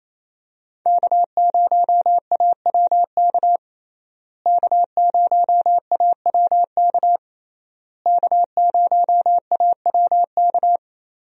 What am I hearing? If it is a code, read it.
Morse code, heard as a keyed tone, about 28 wpm, 700 Hz. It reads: K0AWK K0AWK K0AWK
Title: USA Call Signs - 5 Characters - Repeated 3x - 28wpm